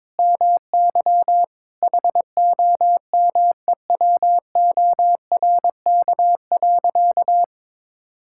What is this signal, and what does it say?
Morse code, heard as a keyed tone, about 22 wpm, 700 Hz. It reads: MY HOMEWORK.
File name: practice / 22wpm / Sentences with 3-Word Phrases - 22wpm